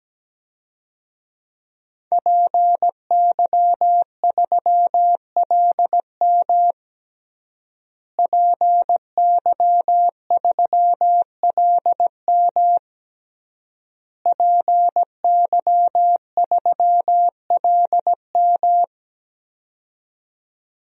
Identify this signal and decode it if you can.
Morse code, heard as a keyed tone, about 17 wpm, 700 Hz. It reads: PY3LM PY3LM PY3LM
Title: DXCC-100 Call Signs - 5 Characters - Repeated 3x - 17wpm